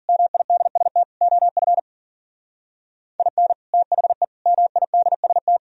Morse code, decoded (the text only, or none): MIDST OF IN THE MIDST